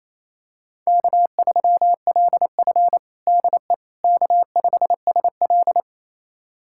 Morse code, heard as a keyed tone, about 28 wpm, 700 Hz. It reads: K3LF DE K5SL
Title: USA Callsign DE Callsign - 4 Characters - 28wpm